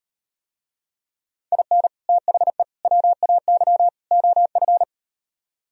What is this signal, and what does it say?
Morse code, heard as a keyed tone, about 38 wpm, 700 Hz. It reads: IN THE WAY OF